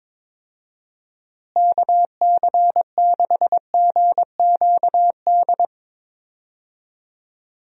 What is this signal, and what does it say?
Morse code, heard as a keyed tone, about 22 wpm, 700 Hz. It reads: KC6GQD
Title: USA Call Signs - 6 Characters - Repeated 2x - 22wpm